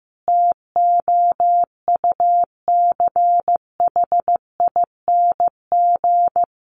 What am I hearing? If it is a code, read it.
Morse code, heard as a keyed tone, about 15 wpm, 700 Hz. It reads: TOUCHING
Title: A Sound Of Thunder 15wpm